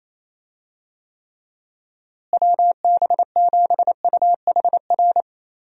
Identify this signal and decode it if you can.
Morse code, heard as a keyed tone, about 28 wpm, 700 Hz. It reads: WB7UHR
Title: USA Call Signs - 6 Characters - Repeated 3x - 28wpm